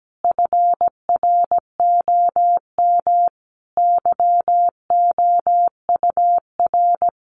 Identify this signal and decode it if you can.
Morse code, heard as a keyed tone, about 17 wpm, 700 Hz. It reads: FROM YOUR
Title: Sentences from Top 1500 Words 17wpm